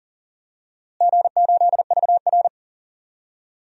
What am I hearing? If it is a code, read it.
Morse code, heard as a keyed tone, about 40 wpm, 700 Hz. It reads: G8VR
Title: DXCC-100 Call Signs - 4 Characters - Repeated 2x - 40wpm